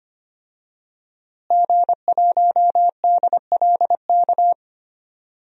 Morse code, heard as a keyed tone, about 25 wpm, 700 Hz. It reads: G1DLK
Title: DXCC-100 Call Signs - 5 Characters - Repeated 2x - 25wpm